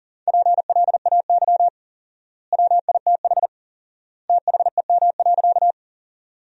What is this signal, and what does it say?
Morse code, heard as a keyed tone, about 37 wpm, 700 Hz. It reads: PLAY WITH THEM.
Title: 4x Word-Spacing - Sentences from Top 200 Words - 40wpm